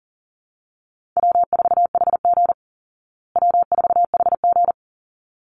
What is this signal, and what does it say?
Morse code, heard as a keyed tone, about 40 wpm, 700 Hz. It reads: W4HG W4HG
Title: Easy Call Signs Repeated 2x - 40wpm